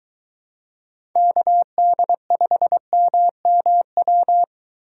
Morse code, heard as a keyed tone, about 22 wpm, 700 Hz. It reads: KD5MMW